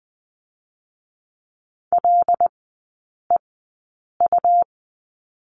Morse code, heard as a keyed tone, about 20 wpm, 700 Hz. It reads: L E U
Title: 100 Lesson 17e - Letter u - Three Characters 20wpm